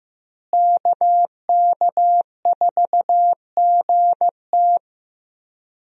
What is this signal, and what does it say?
Morse code, heard as a keyed tone, about 15 wpm, 700 Hz. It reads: KK4GT
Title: USA Call Signs - 5 Characters - Repeated 2x - 15wpm